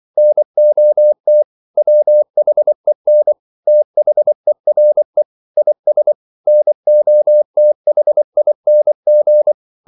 Morse code, heard as a keyed tone, about 24 wpm, 600 Hz. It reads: NOT WHEN THERE IS NOTHING
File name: CW / Practice / EnglishQuotations0224